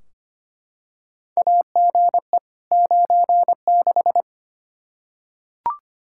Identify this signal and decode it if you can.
Morse code, heard as a keyed tone, about 25 wpm, 700 Hz. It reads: AGE 96
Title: Practice QSO - Exchange 3 - More Info - 25wpm